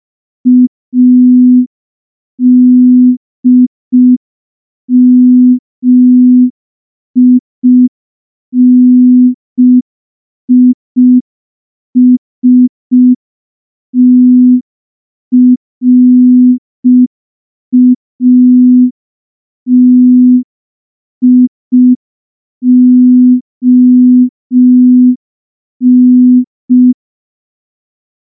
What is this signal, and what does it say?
Morse code, heard as a keyed tone, about 5 wpm, 250 Hz. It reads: ADMINISTRATION
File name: morse / morse-training-05WPM